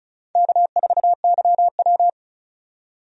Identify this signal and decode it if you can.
Morse code, heard as a keyed tone, about 36 wpm, 700 Hz. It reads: K4YW
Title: USA Call Signs - 4 Characters - 35wpm